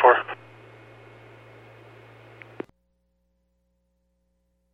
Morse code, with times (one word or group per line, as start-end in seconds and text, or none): none